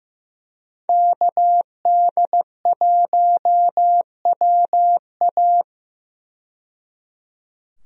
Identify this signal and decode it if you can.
Morse code, heard as a keyed tone, about 15 wpm, 700 Hz. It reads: KD1WA
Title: USA Call Signs - 5 Characters - 15wpm